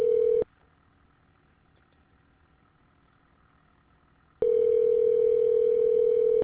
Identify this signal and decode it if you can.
Ringback tone (North America), 440 + 480 Hz.